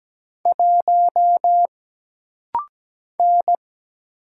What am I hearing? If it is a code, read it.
Morse code, heard as a keyed tone, about 17 wpm, 700 Hz. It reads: 1 N
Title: Single Letter-Number Alt - Rapid-Fire 17wpm